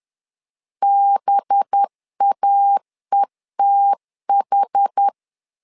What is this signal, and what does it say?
Morse code, heard as a keyed tone, about 10 wpm, 800 Hz. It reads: BAETH
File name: LEZIONI / 5 LBVC / 5 LBV 10wpm 2248